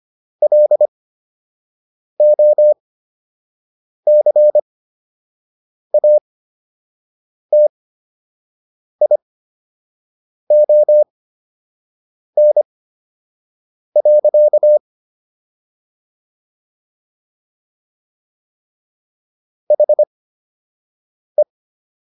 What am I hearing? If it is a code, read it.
Morse code, heard as a keyed tone, about 25 wpm, spaced as for 6 wpm, 600 Hz. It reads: LOCATION. HE